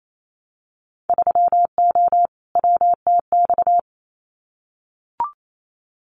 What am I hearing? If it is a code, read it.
Morse code, heard as a keyed tone, about 28 wpm, 700 Hz. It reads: 3O WTX
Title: Winter Field Day Exchanges Encore 28wpm